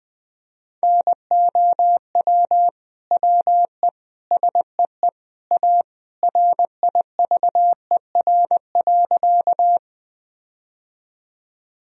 Morse code, heard as a keyed tone, about 20 wpm, 700 Hz. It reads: NOW WE SEE A RIVER.
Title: Sentences from Top 1000 Words - Review Part 1 - 20wpm